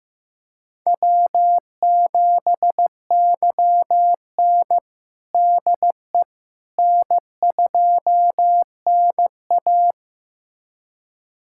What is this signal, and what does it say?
Morse code, heard as a keyed tone, about 15 wpm, 700 Hz. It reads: W7YN DE N2NA
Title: USA Callsign DE Callsign - 4 Characters - 15wpm